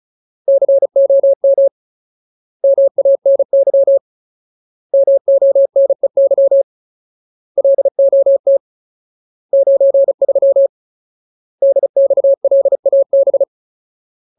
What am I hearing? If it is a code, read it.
Morse code, heard as a keyed tone, about 35 wpm, 550 Hz. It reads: COM MANY MONEY LOT 93 DXLAB
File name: Ultimate Word List / 35WPM Ultimate